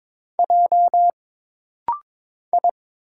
Morse code, heard as a keyed tone, about 22 wpm, 700 Hz. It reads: J I
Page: This rapid-fire timing is geared to push your ICR (Instant Character Recognition) skills.